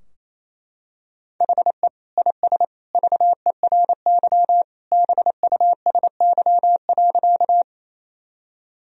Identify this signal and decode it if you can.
Morse code, heard as a keyed tone, about 28 wpm, 700 Hz. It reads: HE IS VERY BUSY.
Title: Sentences from Top 1000 Words - Review Part 3 - 28wpm